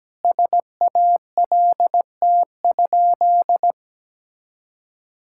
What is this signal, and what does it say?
Morse code, heard as a keyed tone, about 17 wpm, 700 Hz. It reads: SALT?
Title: Sentences from Top 1000 Words - Review Part 5 - 17wpm